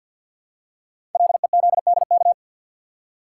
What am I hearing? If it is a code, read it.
Morse code, heard as a keyed tone, about 50 wpm, 690 Hz. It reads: LEZDK
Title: Code Groups - 5 Letters - 50wpm